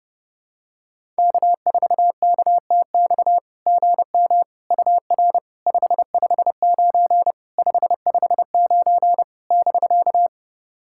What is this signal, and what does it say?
Morse code, heard as a keyed tone, about 30 wpm, 690 Hz. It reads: K4KTX GM UR 559 559 <BK>
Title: SOTA QSO Practice 30wpm